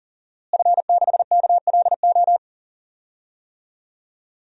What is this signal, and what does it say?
Morse code, heard as a keyed tone, about 40 wpm, 700 Hz. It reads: F6KLO
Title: DXCC-100 Call Signs - 5 Characters - Repeated 3x - 40wpm